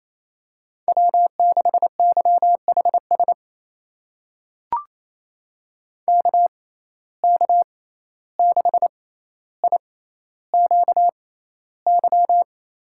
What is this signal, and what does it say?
Morse code, heard as a keyed tone, about 28 wpm, spaced as for 14 wpm, 700 Hz. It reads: W6YHS KK6IQY
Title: USA Call Signs - 6x Character Spacing - 28wpm